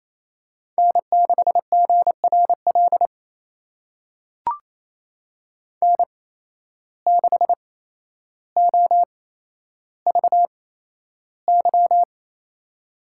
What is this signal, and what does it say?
Morse code, heard as a keyed tone, about 28 wpm, spaced as for 12 wpm, 700 Hz. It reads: N6GRL N6OVY
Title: USA Call Signs - 8x Character Spacing - 28wpm